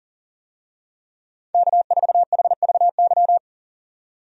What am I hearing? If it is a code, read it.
Morse code, heard as a keyed tone, about 37 wpm, 700 Hz. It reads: K4HVY
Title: USA Call Signs - 5 Characters - 40wpm